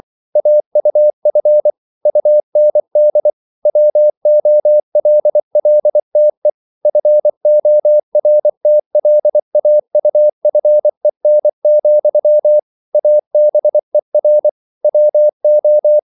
Morse code, heard as a keyed tone, about 24 wpm, 600 Hz. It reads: AUF UND WOLLTE FORTLAUFEN, ABER WO